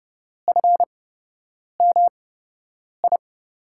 Morse code, heard as a keyed tone, about 30 wpm, spaced as for 8 wpm, 700 Hz. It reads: FMI
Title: USA Call Signs - 5 Characters - 8x Character Spacing - 30wpm